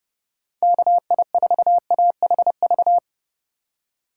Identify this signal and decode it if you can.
Morse code, heard as a keyed tone, about 30 wpm, 700 Hz. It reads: KI4AHV
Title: USA Call Signs - 6 Characters - 3x Character Spacing - 30wpm